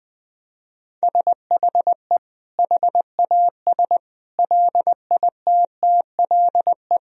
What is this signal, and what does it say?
Morse code, heard as a keyed tone, about 20 wpm, 700 Hz. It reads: SHE HAS LITTLE
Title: Sentences from Top 1000 Words - Review Part 3 - 20wpm